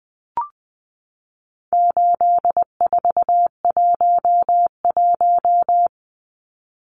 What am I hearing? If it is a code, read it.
Morse code, heard as a keyed tone, about 20 wpm, 700 Hz. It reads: 8411
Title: Numbers - 4-Digit 20wpm